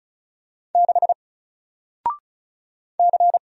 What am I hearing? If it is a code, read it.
Morse code, heard as a keyed tone, about 34 wpm, 700 Hz. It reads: 6 C
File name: Single-Letter-Number-35wpm